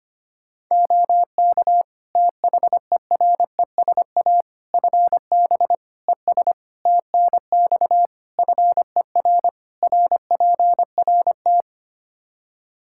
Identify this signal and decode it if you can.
Morse code, heard as a keyed tone, about 25 wpm, 700 Hz. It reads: OK THERESA FB ES TNX FER RPRT
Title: Practice QSO - Exchange 2 - Info - 25wpm